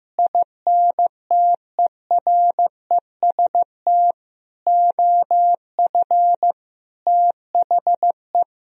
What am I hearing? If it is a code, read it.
Morse code, heard as a keyed tone, about 15 wpm, 700 Hz. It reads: INTEREST OF THE
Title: Sentences from Top 1000 Words - Review Part 4 - 15wpm